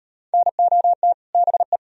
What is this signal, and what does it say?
Morse code, heard as a keyed tone, about 38 wpm, 700 Hz. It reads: NOT BE